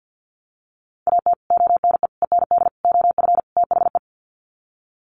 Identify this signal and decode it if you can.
Morse code, heard as a keyed tone, about 50 wpm, 700 Hz. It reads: AT ONE END OF THE
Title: Sets of 5 Words Encore 50wpm